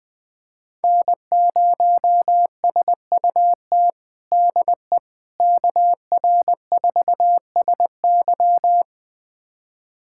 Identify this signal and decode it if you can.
Morse code, heard as a keyed tone, about 20 wpm, 700 Hz. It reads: N0SUT DE KR4SY
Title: USA Callsign DE Callsign - 5 Characters - 20wpm